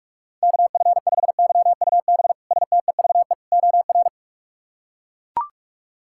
Morse code, heard as a keyed tone, about 41 wpm, 700 Hz. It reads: KF5YUB STEVE OR
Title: K1USN SST QSO Practice - 45wpm